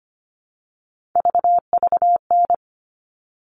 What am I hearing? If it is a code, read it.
Morse code, heard as a keyed tone, about 25 wpm, 700 Hz. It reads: VVN
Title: All Cut Numbers - 3-Digit 25wpm